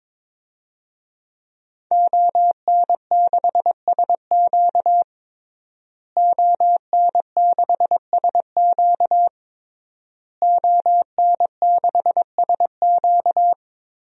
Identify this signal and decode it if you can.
Morse code, heard as a keyed tone, about 22 wpm, 700 Hz. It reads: ON6SQ ON6SQ ON6SQ
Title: DXCC-100 Call Signs - 5 Characters - Repeated 3x - 22wpm